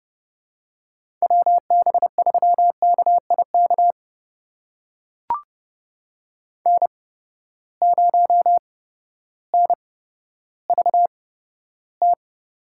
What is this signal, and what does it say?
Morse code, heard as a keyed tone, about 30 wpm, spaced as for 14 wpm, 700 Hz. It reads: WB3KIK N0NVT
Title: USA Call Signs - 8x Character Spacing - 30wpm